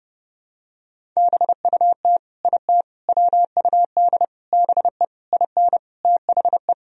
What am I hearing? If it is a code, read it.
Morse code, heard as a keyed tone, about 32 wpm, 700 Hz. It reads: BUT IT WUD BE IN THE
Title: Sentences from Top 300 Words and Abbreviations 30wpm